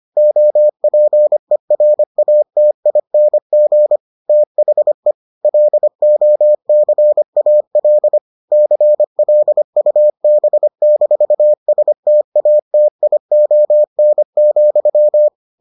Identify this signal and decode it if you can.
Morse code, heard as a keyed tone, about 25 wpm, 600 Hz. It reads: OPERATING THE LOCAL CLUB-STATION,